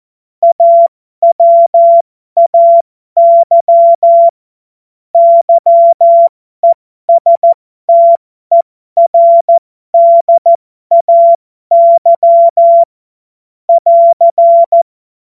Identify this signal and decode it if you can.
Morse code, heard as a keyed tone, about 14 wpm, spaced as for 12 wpm, 680 Hz. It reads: AWAY YESTERDAY <AR>
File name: Morsekurs / Witze